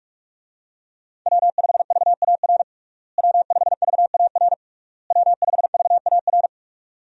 Morse code, heard as a keyed tone, about 46 wpm, 700 Hz. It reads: W5VAR W5VAR W5VAR
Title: USA Call Signs - 5 Characters - Repeated 3x - 45wpm